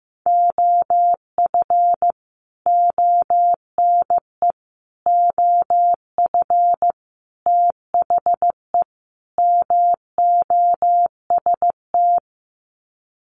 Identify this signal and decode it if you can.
Morse code, heard as a keyed tone, about 15 wpm, 700 Hz. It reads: OF ONE OF THE MOST
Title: Sets of 5 Words Encore2 15wpm